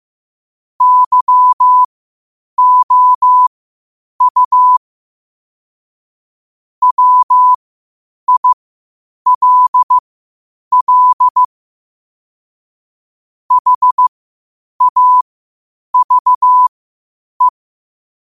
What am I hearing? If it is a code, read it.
Morse code, heard as a keyed tone, about 15 wpm, spaced as for 8 wpm, 1000 Hz. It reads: YOU WILL HAVE
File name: qotdfiles / 8wpm